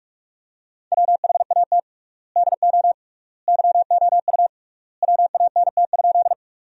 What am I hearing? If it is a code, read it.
Morse code, heard as a keyed tone, about 42 wpm, 700 Hz. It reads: WHAT DO YOU WANT?
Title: 3x Word-Spacing - Sentences from Top 200 Words - 45wpm